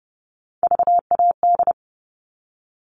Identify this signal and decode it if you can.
Morse code, heard as a keyed tone, about 30 wpm, 700 Hz. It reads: VAD
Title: All Cut Numbers - 3-Digit 30wpm